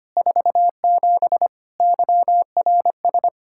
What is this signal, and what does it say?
Morse code, heard as a keyed tone, about 25 wpm, 700 Hz. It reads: 47 YRS
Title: Practice QSO - Exchange 3 - More Info - 25wpm